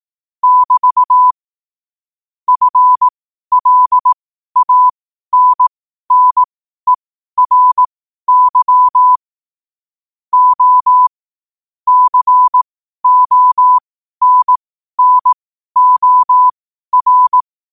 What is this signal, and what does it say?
Morse code, heard as a keyed tone, about 18 wpm, spaced as for 12 wpm, 1000 Hz. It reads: <BT> FLANNERY O CONNOR
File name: qotdfiles / 12wpm